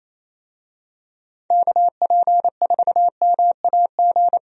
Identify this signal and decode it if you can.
Morse code, heard as a keyed tone, about 28 wpm, 700 Hz. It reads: KP4MAG
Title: USA Call Signs - 6 Characters - Repeated 3x - 28wpm